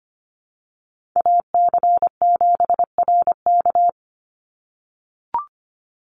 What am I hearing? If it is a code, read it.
Morse code, heard as a keyed tone, about 25 wpm, 700 Hz. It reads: AC7RK
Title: USA Call Signs 25wpm